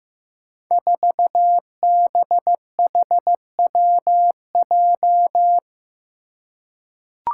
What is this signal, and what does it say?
Morse code, heard as a keyed tone, about 15 wpm, 700 Hz. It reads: 4BHWJ
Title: Five Challenging Letter-Number Combinations 15wpm